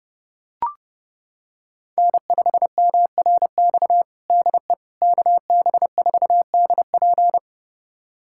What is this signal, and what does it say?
Morse code, heard as a keyed tone, about 30 wpm, 700 Hz. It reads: N5MRX DE KB4DP
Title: USA Callsign DE Callsign - 5 Characters - 30wpm